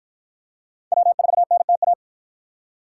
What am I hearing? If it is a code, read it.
Morse code, heard as a keyed tone, about 49 wpm, 700 Hz. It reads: W4NTA